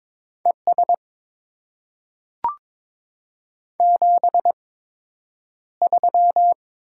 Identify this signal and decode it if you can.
Morse code, heard as a keyed tone, about 22 wpm, spaced as for 9 wpm, 700 Hz. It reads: ES 73
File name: Top 20 QSO Elements - 8x Character Spacing - 22wpm